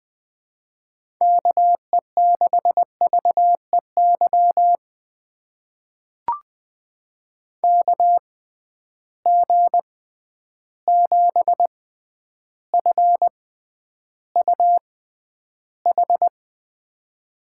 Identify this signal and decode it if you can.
Morse code, heard as a keyed tone, about 20 wpm, spaced as for 10 wpm, 700 Hz. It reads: KE6VEY KG7FUH
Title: USA Call Signs - 6 Characters - 6x Character Spacing - 20wpm